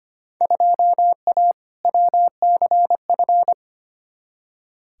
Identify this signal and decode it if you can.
Morse code, heard as a keyed tone, about 25 wpm, 700 Hz. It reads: 2A WCF
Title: Field Day Exchanges - 25wpm